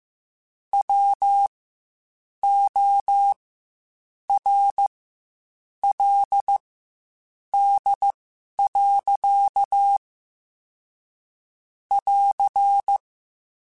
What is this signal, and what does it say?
Morse code, heard as a keyed tone, about 15 wpm, spaced as for 7 wpm, 800 Hz. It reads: WORLD. <AR>
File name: Quotes Spaced / Quotes Spaced-mp3s / ChildAsleep sp @10